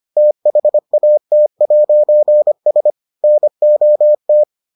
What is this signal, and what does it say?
Morse code, heard as a keyed tone, about 25 wpm, 600 Hz. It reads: THAT'S NOT